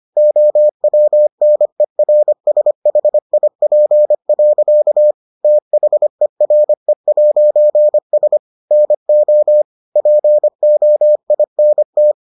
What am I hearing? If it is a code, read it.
Morse code, heard as a keyed tone, about 25 wpm, 600 Hz. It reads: OWNERSHIP. THERE'S NO POINT